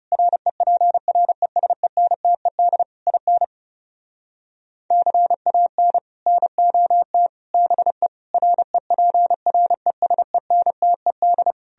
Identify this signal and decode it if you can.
Morse code, heard as a keyed tone, about 40 wpm, 700 Hz. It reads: REPRESENTED IN CAN NOT BE REPRESENTED